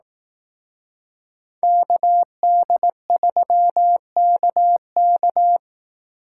Morse code, heard as a keyed tone, about 18 wpm, 700 Hz. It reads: KD3KK
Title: Speed Racing - USA 2x2 Call Signs 15wpm